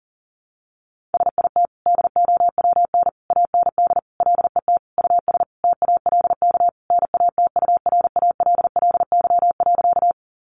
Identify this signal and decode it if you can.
Morse code, heard as a keyed tone, about 40 wpm, 700 Hz. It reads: SIT DOWN AND LET US TALK NATURALLY.
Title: Sentences from Top 1300 Words 40wpm